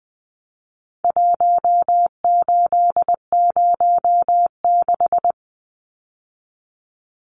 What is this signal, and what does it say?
Morse code, heard as a keyed tone, about 20 wpm, 700 Hz. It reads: 1806
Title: Numbers - 4-Digit 20wpm